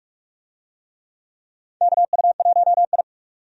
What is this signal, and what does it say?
Morse code, heard as a keyed tone, about 46 wpm, 700 Hz. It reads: KU1I
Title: USA Call Signs - 4 Characters - 45wpm